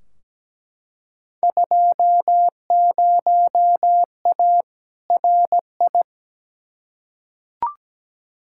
Morse code, heard as a keyed tone, about 17 wpm, 700 Hz. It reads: 20A RI
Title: Field Day Exchanges Encore - 17wpm